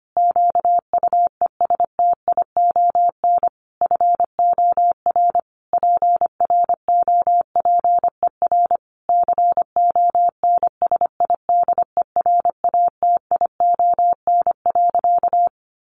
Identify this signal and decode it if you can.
Morse code, heard as a keyed tone, about 25 wpm, 700 Hz. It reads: QUESTION FOR PROPER CONSIDERATION.
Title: Sentences from Top 1500 Words 25wpm